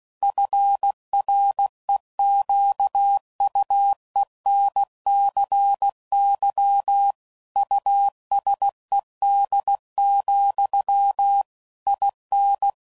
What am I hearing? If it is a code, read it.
Morse code, heard as a keyed tone, about 16 wpm, 800 Hz. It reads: FREQUENCY USED, IN
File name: CW Code Groups @15WPM / Technical Subjects / PwrSupply